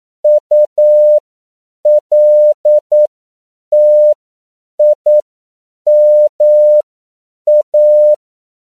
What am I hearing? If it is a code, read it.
Morse code, heard as a keyed tone, about 9 wpm, spaced as for 7 wpm, 600 Hz. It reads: ULTIMA
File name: CW / 10PPM